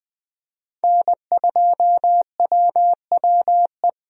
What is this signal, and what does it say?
Morse code, heard as a keyed tone, about 20 wpm, 700 Hz. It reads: N2WWE
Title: USA Call Signs - 5 Characters - 20wpm